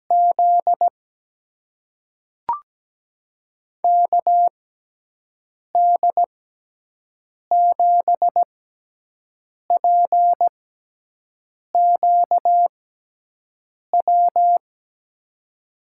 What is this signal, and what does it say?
Morse code, heard as a keyed tone, about 17 wpm, spaced as for 6 wpm, 700 Hz. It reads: Z KD7PQW
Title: USA Call Signs - 6x Character Spacing - 17wpm